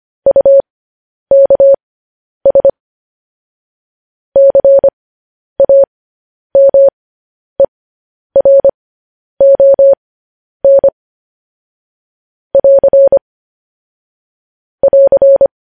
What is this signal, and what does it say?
Morse code, heard as a keyed tone, about 25 wpm, spaced as for 10 wpm, 550 Hz. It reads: UKS CAMERON <AR> <AR>